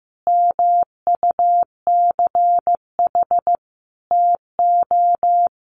Morse code, heard as a keyed tone, about 15 wpm, 700 Hz. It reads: MUCH TO
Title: Sentences from Top 1200 Words 15wpm